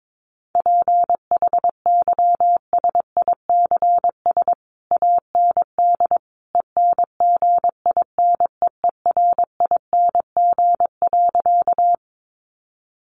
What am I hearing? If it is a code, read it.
Morse code, heard as a keyed tone, about 22 wpm, 700 Hz. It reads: PHYSICS AND ENGINEERING.
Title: Sentences from Top 1500 Words 22wpm